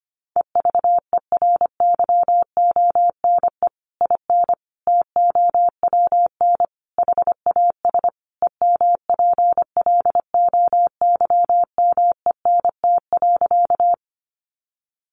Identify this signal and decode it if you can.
Morse code, heard as a keyed tone, about 25 wpm, 700 Hz. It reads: EVERYONE IN TOWN HAS EMPLOYMENT.
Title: Sentences from Top 1500 Words 25wpm